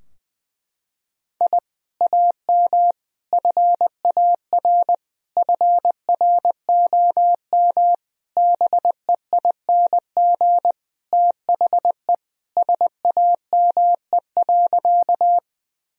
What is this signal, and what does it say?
Morse code, heard as a keyed tone, about 21 wpm, 700 Hz. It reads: I AM FAR FROM BEING THE SAME.
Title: Sentences from Top 1000 Words - Review Part 3 - 20wpm